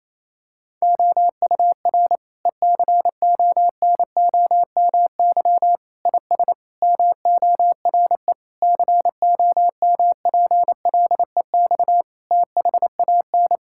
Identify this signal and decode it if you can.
Morse code, heard as a keyed tone, about 28 wpm, 700 Hz. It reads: OUR ECONOMY IS MORE COMPLEX THAN
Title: Sentences from Top 1000 Words - Review Part 6 - 28wpm